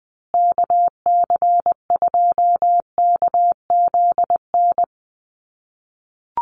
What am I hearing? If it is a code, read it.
Morse code, heard as a keyed tone, about 20 wpm, 700 Hz. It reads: KC2KZN
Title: USA Call Signs Repeated 3x 20wpm